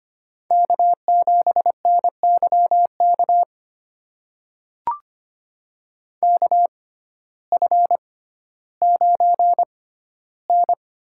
Morse code, heard as a keyed tone, about 25 wpm, spaced as for 14 wpm, 700 Hz. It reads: K7NYK KF9N
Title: USA Call Signs - 5 Characters - 6x Character Spacing - 25wpm